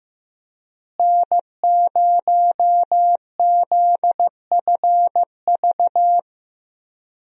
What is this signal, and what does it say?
Morse code, heard as a keyed tone, about 15 wpm, 700 Hz. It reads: N0ZFV
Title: USA Call Signs - 5 Characters - Repeated 2x - 15wpm